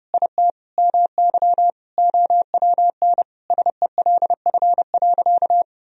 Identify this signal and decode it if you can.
Morse code, heard as a keyed tone, about 30 wpm, 700 Hz. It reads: IT MY OWN SELF.
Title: Sentences from Top 1000 Words - Review Part 1 - 30wpm